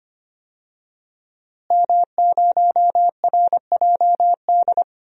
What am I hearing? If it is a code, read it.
Morse code, heard as a keyed tone, about 25 wpm, 700 Hz. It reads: M0RJD
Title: DXCC-100 Call Signs - 5 Characters - Repeated 3x - 25wpm